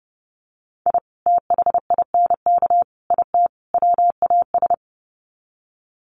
Morse code, heard as a keyed tone, about 30 wpm, 700 Hz. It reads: I THINK IT WAS
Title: Sets of 4 Words 30wpm